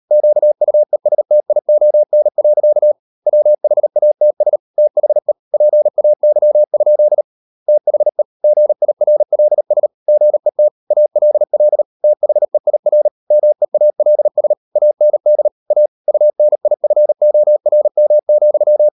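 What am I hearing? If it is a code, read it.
Morse code, heard as a keyed tone, about 38 wpm, 600 Hz. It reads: QUESTION. WHATS THE PAY? THE GIRLS GET ALL THEIR MEALS AND A UNIFORM,
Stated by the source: Morse code practice at 38 WPM, sending Ken Follett's novel Never in English, from CW med Gnister.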